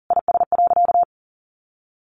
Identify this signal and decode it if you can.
Morse code, heard as a keyed tone, about 40 wpm, 700 Hz. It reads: IS.
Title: Sentences from Top 1000 Words 40wpm